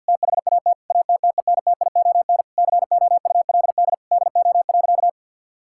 Morse code, heard as a keyed tone, about 50 wpm, 700 Hz. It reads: THAT ATTENTION COULD DO.
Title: Sentences from Top 1000 Words - Review Part 6 - 50wpm